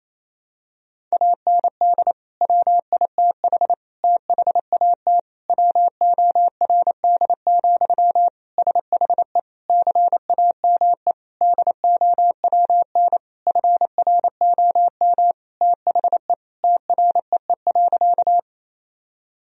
Morse code, heard as a keyed tone, about 27 wpm, 700 Hz. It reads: AND WITH THAT WORD, SHE CAME DOWN FROM THE TREE.
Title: Sentences from Top 1000 Words - Review Part 1 - 28wpm